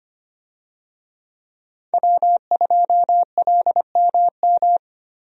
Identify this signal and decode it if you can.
Morse code, heard as a keyed tone, about 25 wpm, 700 Hz. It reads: W2LMM
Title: USA Call Signs - 5 Characters - Repeated 2x - 25wpm